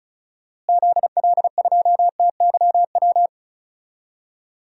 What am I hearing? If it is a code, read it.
Morse code, heard as a keyed tone, about 36 wpm, 700 Hz. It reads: ZL2TYW
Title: DXCC-100 Call Signs - 6 Characters - Repeated 2x - 35wpm